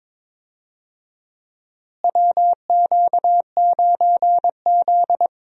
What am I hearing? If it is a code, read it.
Morse code, heard as a keyed tone, about 21 wpm, 700 Hz. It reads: WQ9Z
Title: USA Call Signs - 4 Characters - Repeated 3x - 22wpm